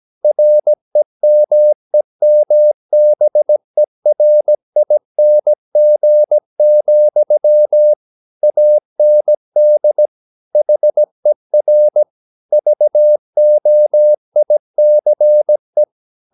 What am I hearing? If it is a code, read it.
Morse code, heard as a keyed tone, about 17 wpm, 600 Hz. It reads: REMEMBERING, AND HER VOICE
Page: Romanen Never af Ken Follet på engelsk.